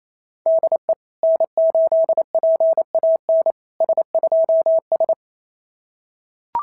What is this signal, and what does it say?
Morse code, heard as a keyed tone, about 27 wpm, 660 Hz. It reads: DE N8PAN S2S
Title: SOTA QSO Practice 28wpm